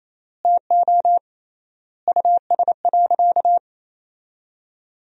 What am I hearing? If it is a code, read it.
Morse code, heard as a keyed tone, about 27 wpm, 700 Hz. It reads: TO US.
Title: 3x Word-Spacing - Sentences from Top 200 Words - 28wpm